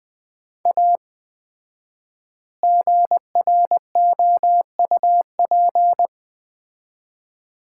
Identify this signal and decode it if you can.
Morse code, heard as a keyed tone, about 20 wpm, 700 Hz. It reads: A GROUP
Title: 4x Word-Spacing - Sets of 3 Words - 20wpm